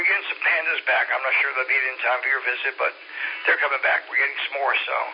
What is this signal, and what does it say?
Emergency Alert System (SAME) tones, heard as harsh buzzing bursts, with speech. It is heard from 0.7 to 4.3 s.